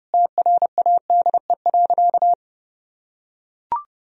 Morse code, heard as a keyed tone, about 32 wpm, 700 Hz. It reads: TRADE.
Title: Sentences from Top 1000 Words - Review Part 4 - 30wpm